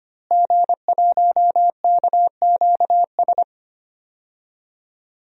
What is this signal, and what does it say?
Morse code, heard as a keyed tone, about 25 wpm, 700 Hz. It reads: G1KQS
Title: DXCC-100 Call Signs - 5 Characters - Repeated 2x - 25wpm